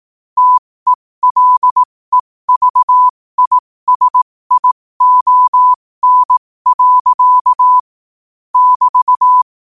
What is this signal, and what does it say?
Morse code, heard as a keyed tone, about 18 wpm, spaced as for 15 wpm, 1000 Hz. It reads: TELEVISION. <BT>
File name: qotdfiles / 15wpm